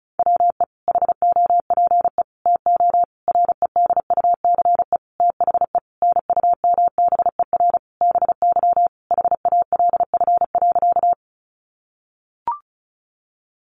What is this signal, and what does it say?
Morse code, heard as a keyed tone, about 34 wpm, 700 Hz. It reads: WE HOPE TO REDUCE THE NUMBER BY HALF.
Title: Sentences from Top 1100 Words 35wpm